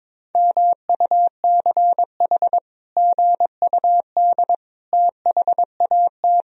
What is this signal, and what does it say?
Morse code, heard as a keyed tone, about 22 wpm, 700 Hz. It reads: MUCH GUD THAT
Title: Sentences from Top 200 Words and Abbreviations 22wpm